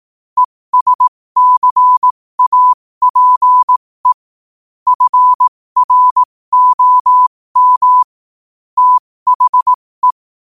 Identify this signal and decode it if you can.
Morse code, heard as a keyed tone, about 18 wpm, spaced as for 15 wpm, 1000 Hz. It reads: ESCAPE FROM THE
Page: New quotes every day in morse code at 15 Words per minute.